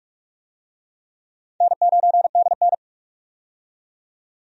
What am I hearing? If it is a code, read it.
Morse code, heard as a keyed tone, about 46 wpm, 700 Hz. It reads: N9DN
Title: USA Call Signs - 4 Characters - 45wpm